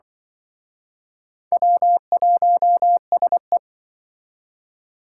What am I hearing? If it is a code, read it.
Morse code, heard as a keyed tone, about 24 wpm, 700 Hz. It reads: W1SE
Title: Speed Racing - USA 1x2 Call Signs 20wpm